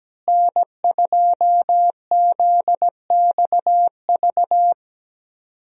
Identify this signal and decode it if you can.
Morse code, heard as a keyed tone, about 17 wpm, 700 Hz. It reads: N2ZXV
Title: USA Call Signs - 5 Characters - Repeated 3x - 17wpm